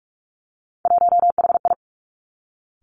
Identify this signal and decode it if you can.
Morse code, heard as a keyed tone, about 46 wpm, 700 Hz. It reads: 1HI
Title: Three Challenging Letter-Number Combinations 45wpm